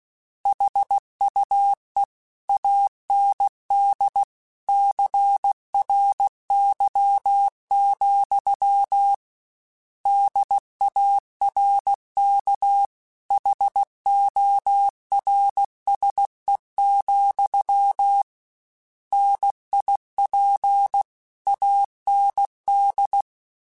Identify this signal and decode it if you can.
Morse code, heard as a keyed tone, about 16 wpm, 800 Hz. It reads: HUE AND CRY, DARK HORSE, NIP AND
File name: Misc / Misc-mp3s / Word Pairs @15